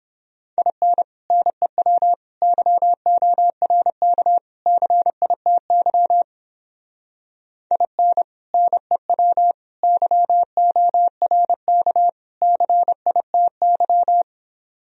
Morse code, heard as a keyed tone, about 31 wpm, 700 Hz. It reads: IN NEW YORK CITY IN NEW YORK CITY